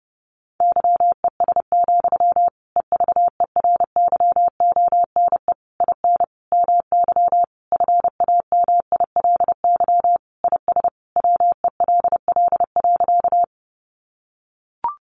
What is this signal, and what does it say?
Morse code, heard as a keyed tone, about 30 wpm, 700 Hz. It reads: YES, EVERYONE IN MY FAMILY IS WELL.
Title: Sentences from Top 800 Words 30wpm